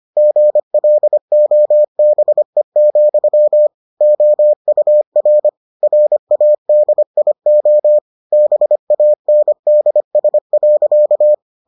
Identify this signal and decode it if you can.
Morse code, heard as a keyed tone, about 25 wpm, 600 Hz. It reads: GLOBE, OUR RADIO BANDS.